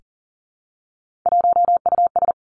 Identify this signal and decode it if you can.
Morse code, heard as a keyed tone, about 39 wpm, 700 Hz. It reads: 1US
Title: Three Challenging Letter-Number Combinations 40wpm